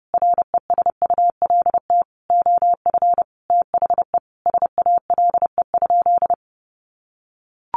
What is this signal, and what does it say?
Morse code, heard as a keyed tone, about 30 wpm, 700 Hz. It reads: RESULT OF THE SALE?
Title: Sentences from Top 1400 Words 30wpm